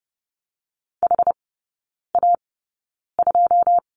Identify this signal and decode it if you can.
Morse code, heard as a keyed tone, about 30 wpm, spaced as for 9 wpm, 700 Hz. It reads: HA2
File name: 081 Lesson 14e - Number 2 - Three Characters 30wpm